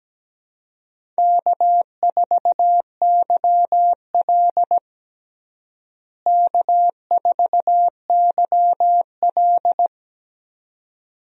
Morse code, heard as a keyed tone, about 17 wpm, 700 Hz. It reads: K4YL K4YL
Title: USA Call Signs - 4 Characters - Repeated 2x - 17wpm